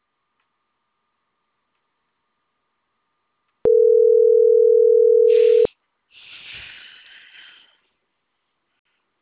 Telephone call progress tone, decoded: ringback tone

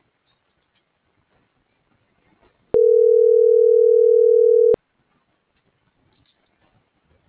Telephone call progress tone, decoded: ringback tone

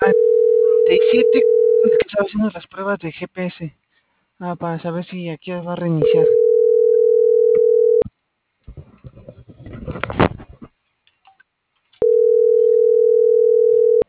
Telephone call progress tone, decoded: ringback tone

